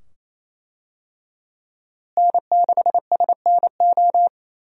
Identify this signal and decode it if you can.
Morse code, heard as a keyed tone, about 28 wpm, 700 Hz. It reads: N6SNO
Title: USA Call Signs - 5 Characters - Repeated 2x - 28wpm